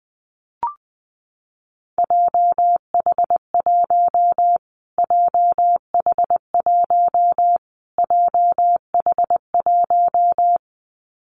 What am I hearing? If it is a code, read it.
Morse code, heard as a keyed tone, about 20 wpm, 700 Hz. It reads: JH1 JH1 JH1
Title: Callsign Prefixes for Top 25 DXCC - Repeated 3x - 20wpm